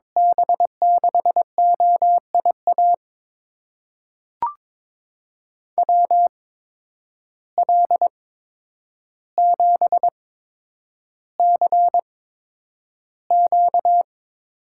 Morse code, heard as a keyed tone, about 22 wpm, spaced as for 9 wpm, 700 Hz. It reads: B6OIA WL7CQ
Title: USA Call Signs - 6 Characters - 8x Character Spacing - 22wpm